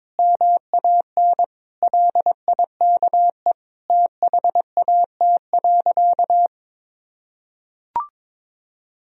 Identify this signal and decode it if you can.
Morse code, heard as a keyed tone, about 22 wpm, 700 Hz. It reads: MAN LIKE THAT.
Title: Sentences from Top 1000 Words - Review Part 4 - 22wpm